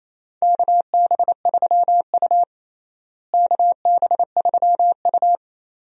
Morse code, heard as a keyed tone, about 28 wpm, 700 Hz. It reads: KB3U KB3U
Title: USA Call Signs - 4 Characters - Repeated 2x - 28wpm